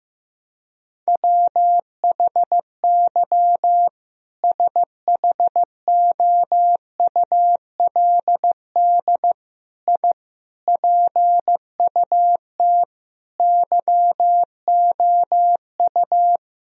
Morse code, heard as a keyed tone, about 16 wpm, 700 Hz. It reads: WHY SHOULD I PUT YOU
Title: Sentences from Top 1000 Words - Review Part 4 - 15wpm